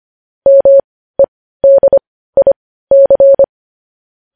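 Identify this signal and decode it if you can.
Morse code, heard as a keyed tone, about 25 wpm, spaced as for 15 wpm, 550 Hz. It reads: MEDIC
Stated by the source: Morse code news headlines practice at 15, preceded by vvv.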